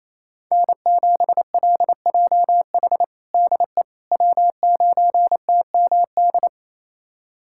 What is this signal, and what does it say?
Morse code, heard as a keyed tone, about 28 wpm, 700 Hz. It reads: N7LJH DE W9TMD
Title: USA Callsign DE Callsign - 5 Characters - 28wpm